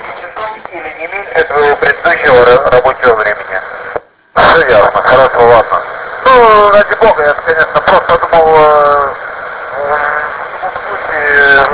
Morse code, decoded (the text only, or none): none